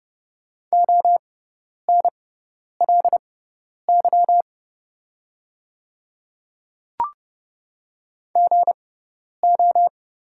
Morse code, heard as a keyed tone, about 30 wpm, spaced as for 10 wpm, 700 Hz. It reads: ONLY GO
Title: 6x Character Spacing - Top 200 Words - 30wpm